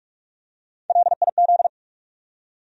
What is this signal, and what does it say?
Morse code, heard as a keyed tone, about 47 wpm, 700 Hz. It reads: LIZ
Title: USA Common Names - 45wpm